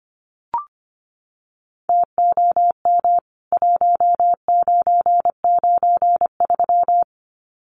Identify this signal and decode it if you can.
Morse code, heard as a keyed tone, about 25 wpm, 700 Hz. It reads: TOM 1993
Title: CWT Exchange 25wpm